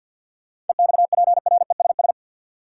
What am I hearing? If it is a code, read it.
Morse code, heard as a keyed tone, about 51 wpm, 700 Hz. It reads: EXPRESS
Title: Seven Letter Words - 50wpm